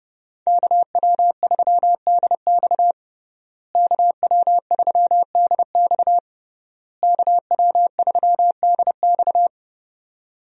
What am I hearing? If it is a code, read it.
Morse code, heard as a keyed tone, about 32 wpm, 700 Hz. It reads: KW3DX KW3DX KW3DX
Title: USA Call Signs - 5 Characters - Repeated 3x - 30wpm